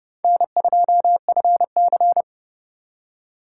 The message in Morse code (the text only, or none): N2FC